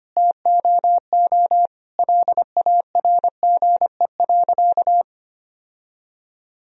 Morse code, heard as a keyed tone, about 25 wpm, 700 Hz. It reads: TOO LARGE.
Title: Sentences from Top 1000 Words - Review Part 1 - 25wpm